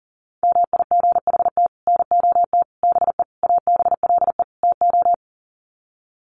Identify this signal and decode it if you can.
Morse code, heard as a keyed tone, about 37 wpm, 700 Hz. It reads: MIGHT NOT BE ABLE TO
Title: Sets of 5 Words Encore2 40wpm